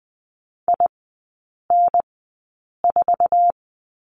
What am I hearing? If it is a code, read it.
Morse code, heard as a keyed tone, about 20 wpm, spaced as for 8 wpm, 700 Hz. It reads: IN4
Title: 047 Lesson 9e - Number 4 - Three Characters 20wpm